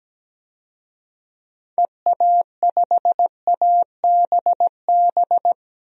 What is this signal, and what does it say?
Morse code, heard as a keyed tone, about 17 wpm, 700 Hz. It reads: EA5ABB
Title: DXCC-100 Call Signs - 6 Characters - 17wpm